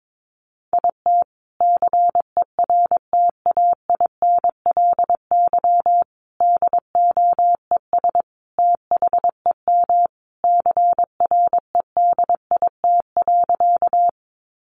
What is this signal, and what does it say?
Morse code, heard as a keyed tone, about 22 wpm, 700 Hz. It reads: IT CERTAINLY DOES THEM CREDIT.
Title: Sentences from Top 1000 Words 22wpm